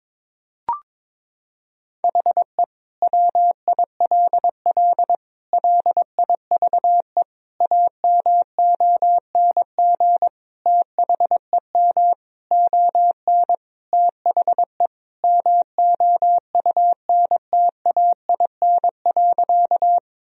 Morse code, heard as a keyed tone, about 22 wpm, 700 Hz. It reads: HE WILL LIVE AMONG THEM ON THE MOUNTAIN.
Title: Sentences from Top 1000 Words - Review Part 3 - 22wpm